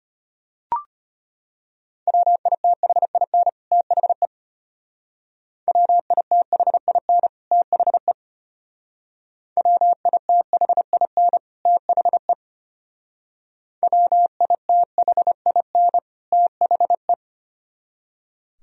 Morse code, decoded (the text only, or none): WITHIN THE WITHIN THE WITHIN THE WITHIN THE